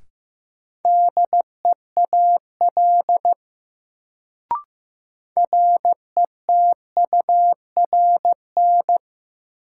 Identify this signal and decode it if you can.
Morse code, heard as a keyed tone, about 15 wpm, 700 Hz. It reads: DEAL RETURN
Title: Sentences from Top 800 Words - New Word Drill - 15wpm